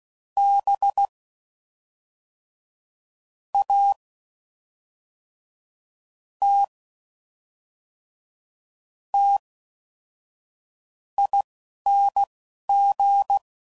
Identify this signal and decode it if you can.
Morse code, heard as a keyed tone, about 16 wpm, spaced as for 5 wpm, 800 Hz. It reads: BATTING